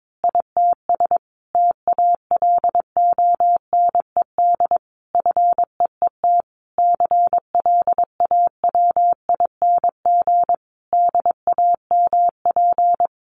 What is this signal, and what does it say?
Morse code, heard as a keyed tone, about 22 wpm, 700 Hz. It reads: ITS TALONED FEET CLAWING DAMP
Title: A Sound Of Thunder 22wpm